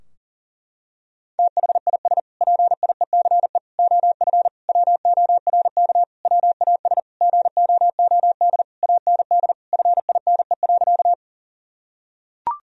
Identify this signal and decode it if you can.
Morse code, heard as a keyed tone, about 37 wpm, 700 Hz. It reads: THIS PIECE OF WORK WAS GOOD AND FINE.
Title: Sentences from Top 1000 Words - Review Part 2 - 40wpm